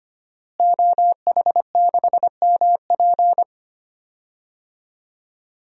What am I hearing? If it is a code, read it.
Morse code, heard as a keyed tone, about 25 wpm, 700 Hz. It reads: OH6MP
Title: DXCC-100 Call Signs - 5 Characters - Repeated 3x - 25wpm